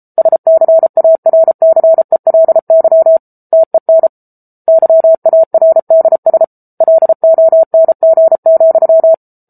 Morse code, heard as a keyed tone, about 34 wpm, 650 Hz. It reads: SCARCELY TEN YARDS LONG,
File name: Book Two / 07-Chapter 7 - 33 WPM